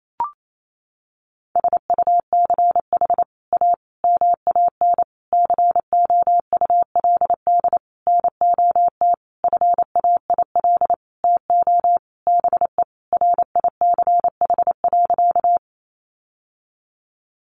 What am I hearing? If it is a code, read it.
Morse code, heard as a keyed tone, about 28 wpm, 700 Hz. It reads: SUCH A MAN COULD NOT FAIL TO BE RICH.
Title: Sentences from Top 1100 Words 28wpm